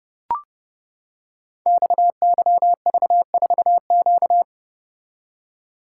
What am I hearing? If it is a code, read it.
Morse code, heard as a keyed tone, about 30 wpm, 700 Hz. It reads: XYV4Q
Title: Code Groups - 5 Letter:Numbers - 30wpm